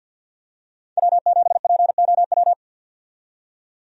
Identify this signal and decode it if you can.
Morse code, heard as a keyed tone, about 52 wpm, 700 Hz. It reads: W7POW
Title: USA Call Signs - 5 Characters - 50wpm